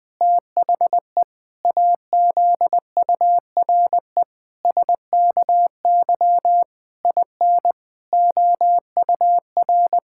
Morse code, heard as a keyed tone, about 19 wpm, 700 Hz. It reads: THE AZURE SKY IN OUR
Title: Pangram Challenge v2 - 20wpm